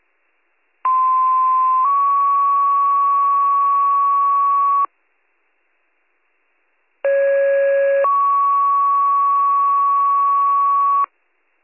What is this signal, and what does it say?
Two-tone page, heard as a one-second tone, then a three-second tone, about 1040 then 1120 Hz, about 550 then 1120 Hz. It is heard from 0.8 to 11.1 s.